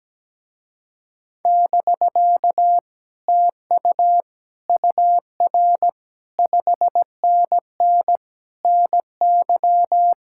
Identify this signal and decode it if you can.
Morse code, heard as a keyed tone, about 17 wpm, 700 Hz. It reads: <BK> TU UR 5NN NY
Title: Speed-Racing Common QSO Phrases Encore - 17wpm